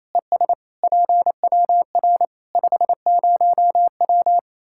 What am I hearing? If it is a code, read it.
Morse code, heard as a keyed tone, about 28 wpm, 700 Hz. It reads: ES PWR 50W
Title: Practice QSO - Exchange 2 - Info - 28wpm